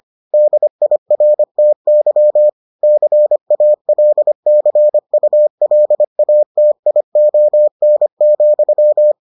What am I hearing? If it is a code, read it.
Morse code, heard as a keyed tone, about 25 wpm, 600 Hz. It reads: DIRTY CALCULATION,